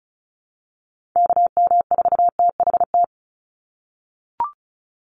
Morse code, heard as a keyed tone, about 34 wpm, 700 Hz. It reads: KM4THT
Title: USA Call Signs 35wpm